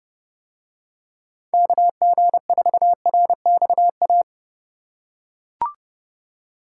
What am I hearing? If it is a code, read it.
Morse code, heard as a keyed tone, about 28 wpm, 700 Hz. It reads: KG4RXA